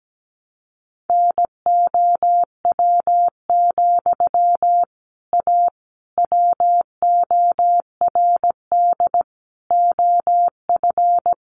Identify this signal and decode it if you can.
Morse code, heard as a keyed tone, about 17 wpm, 700 Hz. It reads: NOW, A WORD OF
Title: Sentences from Top 1100 Words 17wpm